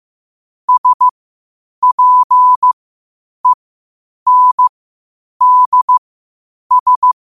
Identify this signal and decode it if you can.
Morse code, heard as a keyed tone, about 15 wpm, spaced as for 8 wpm, 1000 Hz. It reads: SPENDS
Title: 8 WPM morse code quotes for Fri, 01 Mar 2024